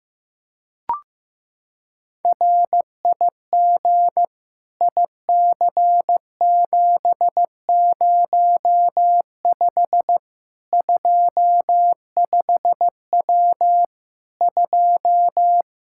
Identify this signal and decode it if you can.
Morse code, heard as a keyed tone, about 15 wpm, 700 Hz. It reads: RIG IC705 25W 2
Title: Practice QSO - Exchange 2 - Info - 15wpm